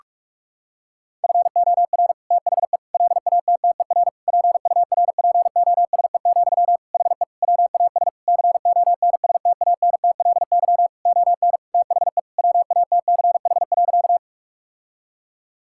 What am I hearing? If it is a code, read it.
Morse code, heard as a keyed tone, about 46 wpm, 700 Hz. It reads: FOR THE LATTER PURPOSE, HE WAS CONSTANTLY ON THE WATCH.
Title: Sentences from Top 1600 Words 45wpm